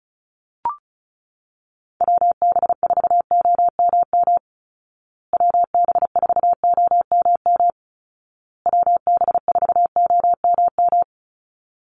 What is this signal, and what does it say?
Morse code, heard as a keyed tone, about 34 wpm, 700 Hz. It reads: WB4OMM WB4OMM WB4OMM
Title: Call Signs Repeated 2x - 35wpm